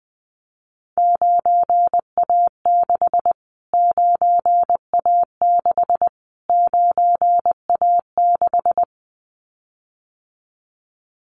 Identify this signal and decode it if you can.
Morse code, heard as a keyed tone, about 20 wpm, 700 Hz. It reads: 9A6 9A6 9A6
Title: Callsign Prefixes for Top 25 DXCC - Repeated 3x - 20wpm